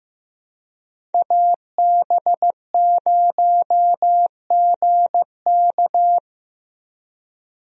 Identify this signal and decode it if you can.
Morse code, heard as a keyed tone, about 15 wpm, 700 Hz. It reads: AB0GK
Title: USA Call Signs - 5 Characters - Repeated 2x - 15wpm